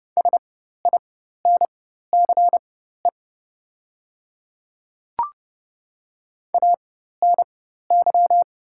Morse code, heard as a keyed tone, about 30 wpm, spaced as for 14 wpm, 700 Hz. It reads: SINCE ANY
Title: 4x Character Spacing - Top 300 Words - 30wpm